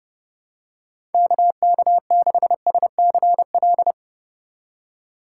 Morse code, heard as a keyed tone, about 29 wpm, 700 Hz. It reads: KK6SCL